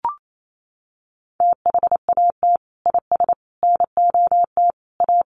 Morse code, heard as a keyed tone, about 28 wpm, 700 Hz. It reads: THAT IS NOT A
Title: Sentences from Top 1400 Words 28wpm